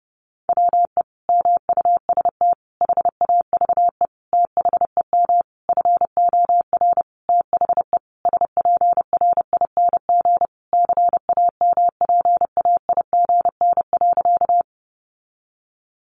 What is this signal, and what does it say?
Morse code, heard as a keyed tone, about 30 wpm, 700 Hz. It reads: WE MUST HAVE THEM FOR THE SPRING CAMPAIGN.
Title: Sentences from Top 1200 Words 30wpm